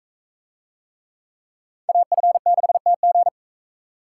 Morse code, heard as a keyed tone, about 42 wpm, 700 Hz. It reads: AF6TG